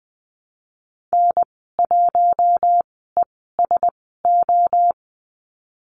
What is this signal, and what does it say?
Morse code, heard as a keyed tone, about 20 wpm, spaced as for 14 wpm, 700 Hz. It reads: N1ESO
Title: 069 Lesson 12g - Letter d - Call Signs 20wpm